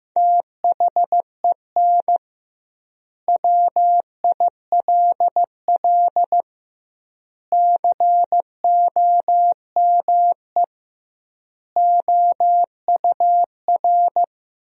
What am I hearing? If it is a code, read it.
Morse code, heard as a keyed tone, about 15 wpm, 700 Hz. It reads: THEN WILL COME OUR
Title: 2x Word-Spacing - Sentences from Top 200 Words - 15wpm